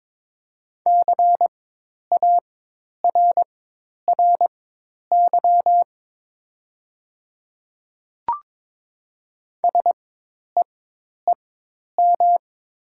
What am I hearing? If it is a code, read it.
Morse code, heard as a keyed tone, about 22 wpm, spaced as for 10 wpm, 700 Hz. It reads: CARRY SEEM
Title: 4x Character Spacing - Top 300 Words - 22wpm